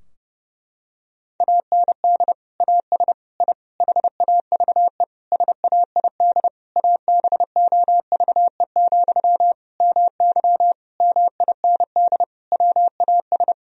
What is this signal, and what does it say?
Morse code, heard as a keyed tone, about 32 wpm, 700 Hz. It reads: AND AS I HAVE SAID ABOVE, MY MIND WAS
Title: Sentences from Top 1000 Words - Review Part 2 - 30wpm